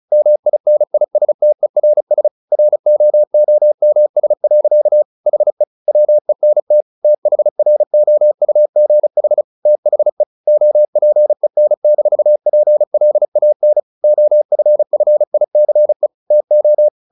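Morse code, heard as a keyed tone, about 34 wpm, 600 Hz. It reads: MINISTERS ROOMS. HE WENT THROUGH THE OPEN-PLAN OFFICE TO
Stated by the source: Morse practice at 35 wpm from CW med Gnister, sending Ken Follett's novel Never in English.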